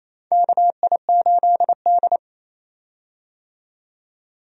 Morse code, heard as a keyed tone, about 28 wpm, 700 Hz. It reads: KI8D
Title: USA Call Signs - 4 Characters - 28wpm